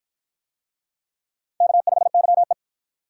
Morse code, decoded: K5YE

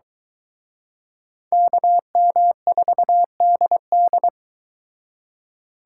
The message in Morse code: KM4DD

